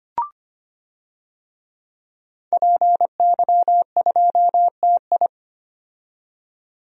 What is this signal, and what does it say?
Morse code, heard as a keyed tone, about 25 wpm, 700 Hz. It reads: PY2TI
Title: DXCC-100 Call Signs - 5 Characters - 25wpm